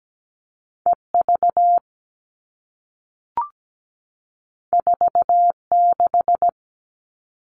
Morse code, heard as a keyed tone, about 17 wpm, 700 Hz. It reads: EV 46
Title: All Cut Numbers - 2-Digit 17wpm